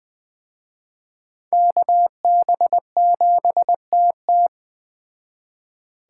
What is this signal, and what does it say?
Morse code, heard as a keyed tone, about 20 wpm, 700 Hz. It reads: KB7TT
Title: Speed Racing - USA 2x2 Call Signs 17wpm